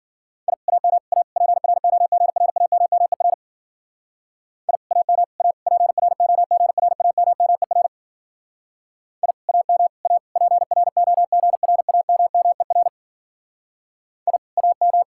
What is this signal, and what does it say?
Morse code, heard as a keyed tone, about 48 wpm, 700 Hz. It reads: I AM A PROGRAMMER I AM A PROGRAMMER I AM A PROGRAMMER I AM
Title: Speed-Racing Common QSO Phrases - 40wpm